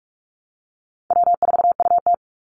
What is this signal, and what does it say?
Morse code, heard as a keyed tone, about 44 wpm, 700 Hz. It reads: W4UT
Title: Easy Call Signs Repeated 2x - 45wpm